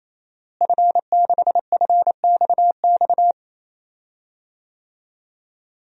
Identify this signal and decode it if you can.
Morse code, heard as a keyed tone, about 28 wpm, 700 Hz. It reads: F6FXX
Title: DXCC-100 Call Signs - 5 Characters - Repeated 3x - 28wpm